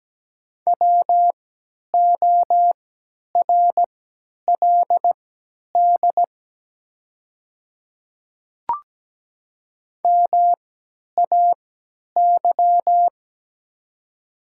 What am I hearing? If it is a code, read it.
Morse code, heard as a keyed tone, about 17 wpm, spaced as for 10 wpm, 700 Hz. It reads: WORLD MAY
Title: 3x Character Spacing - Top 300 Words - 17wpm